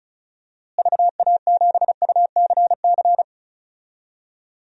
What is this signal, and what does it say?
Morse code, heard as a keyed tone, about 34 wpm, 700 Hz. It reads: VA7UCC
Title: DXCC-100 Call Signs - 6 Characters - Repeated 2x - 35wpm